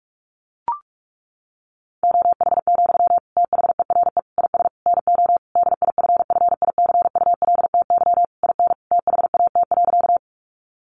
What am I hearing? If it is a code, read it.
Morse code, heard as a keyed tone, about 44 wpm, 700 Hz. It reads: OH, THERE IS NO DIFFICULTY IN THAT.
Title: Sentences from Top 1500 Words 45wpm